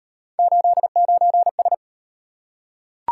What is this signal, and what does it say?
Morse code, heard as a keyed tone, about 36 wpm, 700 Hz. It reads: 89S